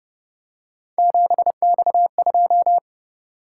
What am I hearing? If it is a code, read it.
Morse code, heard as a keyed tone, about 29 wpm, 700 Hz. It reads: 7X2